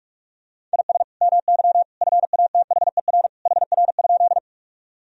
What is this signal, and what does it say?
Morse code, heard as a keyed tone, about 42 wpm, 700 Hz. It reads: IS MY FATHER HR?
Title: Sentences from Top 300 Words and Abbreviations 45wpm